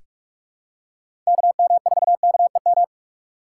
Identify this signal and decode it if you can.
Morse code, heard as a keyed tone, about 41 wpm, 700 Hz. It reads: KM4KEM
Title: USA Call Signs - 3x Character Spacing - 45wpm